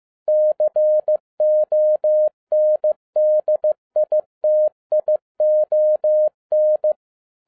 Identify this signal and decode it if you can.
Morse code, heard as a keyed tone, about 15 wpm, 610 Hz. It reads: CONDITION
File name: 2011-07-mp3 / n22 QRP OTAKU 15wpm